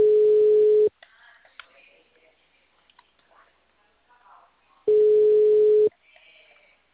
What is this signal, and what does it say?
Ringback tone (Europe, 425 Hz), with speech.